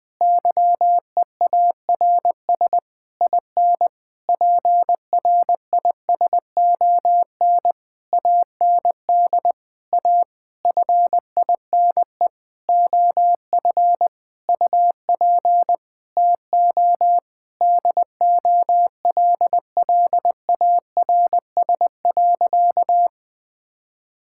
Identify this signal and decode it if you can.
Morse code, heard as a keyed tone, about 20 wpm, 700 Hz. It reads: YEARS IN PRISON AND A FINE OF UP TO DOLLARS.
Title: Sentences with 9-Word Phrases - 20wpm